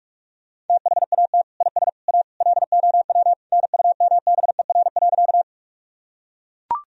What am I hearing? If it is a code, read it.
Morse code, heard as a keyed tone, about 43 wpm, 700 Hz. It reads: THAT IS A LOW NUMBER.
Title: Sentences from Top 1000 Words - Review Part 1 - 45wpm